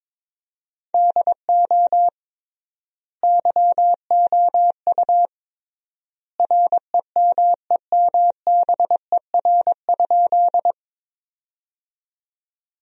Morse code, heard as a keyed tone, about 21 wpm, 700 Hz. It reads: DO YOU REMEMBER?
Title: 3x Word-Spacing - Sentences from Top 400 Words - 22wpm